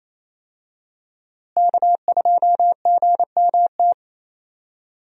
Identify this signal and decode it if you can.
Morse code, heard as a keyed tone, about 27 wpm, 700 Hz. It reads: K2GMT
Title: USA Call Signs - 5 Characters - Repeated 3x - 28wpm